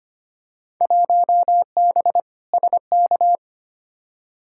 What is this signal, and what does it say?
Morse code, heard as a keyed tone, about 25 wpm, 700 Hz. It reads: 1B SK
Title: Field Day Exchanges Encore - 25wpm